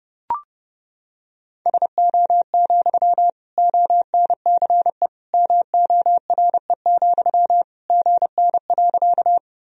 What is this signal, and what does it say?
Morse code, heard as a keyed tone, about 30 wpm, 700 Hz. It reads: SO, ONCE MORE, GN.
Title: Sentences from Top 300 Words and Abbreviations 30wpm